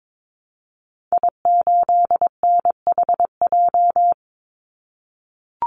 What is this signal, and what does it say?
Morse code, heard as a keyed tone, about 22 wpm, 700 Hz. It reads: I8NHJ
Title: Call Signs - 22wpm